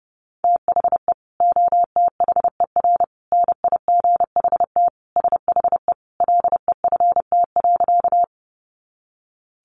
Morse code, heard as a keyed tone, about 30 wpm, 700 Hz. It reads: THE OTHER NIGHT SHE LEFT.
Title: Sentences from Top 300 Words 30wpm